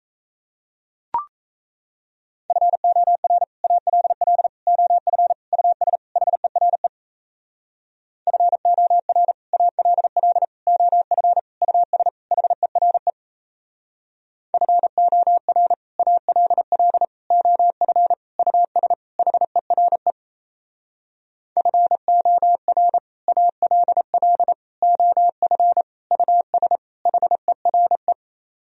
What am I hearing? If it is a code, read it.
Morse code, heard as a keyed tone, about 38 wpm, 700 Hz. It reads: FOR ALL OF US HERE FOR ALL OF US HERE FOR ALL OF US HERE FOR ALL OF US HERE